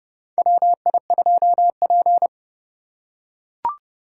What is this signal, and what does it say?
Morse code, heard as a keyed tone, about 31 wpm, 700 Hz. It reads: WI2P
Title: USA Call Signs - 4 Characters - 4x Character Spacing - 30wpm